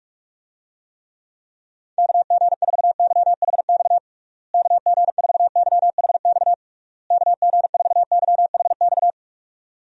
Morse code, heard as a keyed tone, about 42 wpm, 700 Hz. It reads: KG4YHX KG4YHX KG4YHX
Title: USA Call Signs - 6 Characters - Repeated 3x - 45wpm